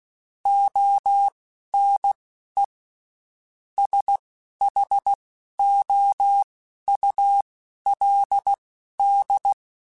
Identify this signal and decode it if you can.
Morse code, heard as a keyed tone, about 16 wpm, spaced as for 11 wpm, 800 Hz. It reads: ONE SHOULD
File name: Quotes Spaced / Quotes Spaced-mp3s / Advice sp @15